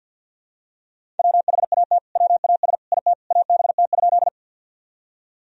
Morse code, heard as a keyed tone, about 50 wpm, 700 Hz. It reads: WHAT WAS IT ABT?
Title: Sentences from Top 100 Words and Abbreviations 50wpm